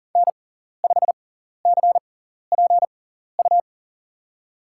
Morse code, heard as a keyed tone, about 40 wpm, spaced as for 14 wpm, 700 Hz. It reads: N5CPU
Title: USA Call Signs - 5 Characters - 6x Character Spacing - 40wpm